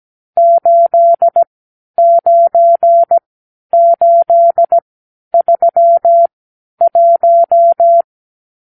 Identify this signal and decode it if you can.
Morse code, heard as a keyed tone, about 17 wpm, spaced as for 10 wpm, 680 Hz. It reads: 89831
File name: arrl / Review (1-0)